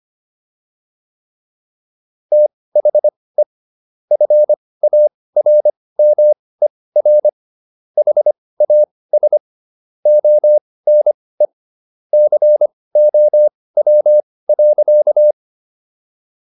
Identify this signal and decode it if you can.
Morse code, heard as a keyed tone, about 25 wpm, spaced as for 18 wpm, 600 Hz. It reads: THE FARMER HAS ONE COW.